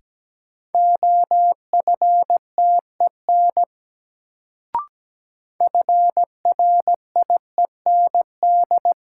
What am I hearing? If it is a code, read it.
Morse code, heard as a keyed tone, about 17 wpm, 700 Hz. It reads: OFTEN FRIEND
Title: Sentences from Top 300 Words - New Word Drill - 17wpm